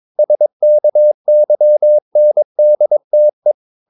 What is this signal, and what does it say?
Morse code, heard as a keyed tone, about 22 wpm, 600 Hz. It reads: SKYNDTE
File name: Ferie QRS 22wpm